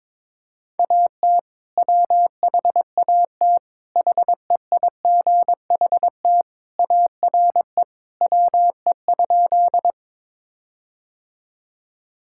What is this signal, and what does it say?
Morse code, heard as a keyed tone, about 21 wpm, 700 Hz. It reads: AT WHAT HEIGHT ARE WE?
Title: Sentences from Top 1600 Words 22wpm